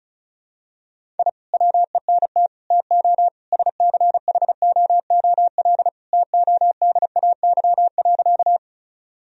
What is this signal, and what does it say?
Morse code, heard as a keyed tone, about 36 wpm, 700 Hz. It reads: I WENT TO SCHOOL TODAY.
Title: Sentences from Top 1000 Words - Review Part 1 - 35wpm